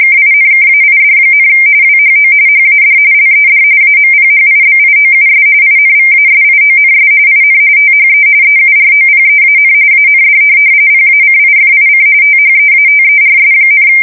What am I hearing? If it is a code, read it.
RTTY, about 45.45 baud. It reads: KINKAID, WHO HAD LED AMPHIBIOUS INVASIONS IN THE EUROPEAN AND PACIFIC THEATERS,
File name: fsk170 / USS-New-Jersey